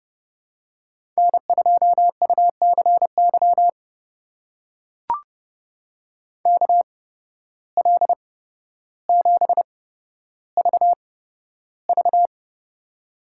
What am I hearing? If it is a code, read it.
Morse code, heard as a keyed tone, about 31 wpm, spaced as for 13 wpm, 700 Hz. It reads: N2UCY KL7VV
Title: USA Call Signs - 5 Characters - 8x Character Spacing - 30wpm